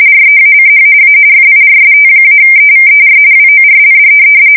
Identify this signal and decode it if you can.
RTTY, about 45.45 baud. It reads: REACH A FIRST-EVER WORLD
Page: 170 Shift 2125/2295